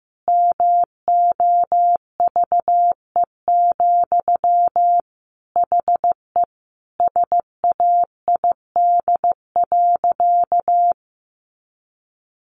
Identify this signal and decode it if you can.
Morse code, heard as a keyed tone, about 15 wpm, 700 Hz. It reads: MOVE, HE SAID.
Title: Sentences from Top 300 Words 15wpm